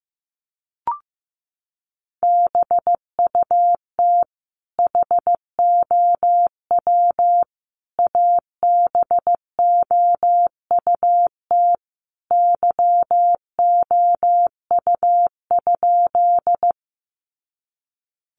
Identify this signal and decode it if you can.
Morse code, heard as a keyed tone, about 15 wpm, 700 Hz. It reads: BUT HOW ABOUT YOU?
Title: Sentences from Top 100 Words 15wpm